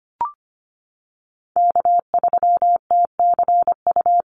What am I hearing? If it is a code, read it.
Morse code, heard as a keyed tone, about 25 wpm, 700 Hz. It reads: K3TCU
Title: NA SOTA Chaser Callsigns 25wpm